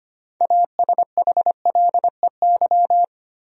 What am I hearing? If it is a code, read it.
Morse code, heard as a keyed tone, about 25 wpm, 700 Hz. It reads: ASHLEY
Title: USA Common Names - 25wpm